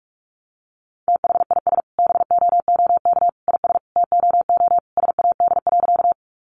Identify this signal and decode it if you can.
Morse code, heard as a keyed tone, about 44 wpm, 700 Hz. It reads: THIS BOOK IS TOO SAD.
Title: Additional Easy Phrases 45wpm